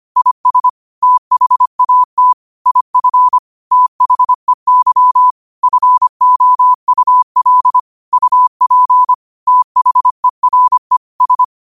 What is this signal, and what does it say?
Morse code, heard as a keyed tone, about 25 wpm, 1000 Hz. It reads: IS THAT IF THEY FOUL UP THERE S
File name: qotdfiles / 25wpm